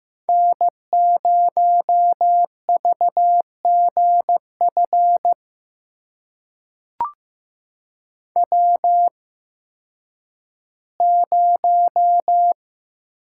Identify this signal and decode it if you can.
Morse code, heard as a keyed tone, about 15 wpm, spaced as for 10 wpm, 700 Hz. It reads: N0VGF W0
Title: USA Call Signs - 5 Characters - 8x Character Spacing - 15wpm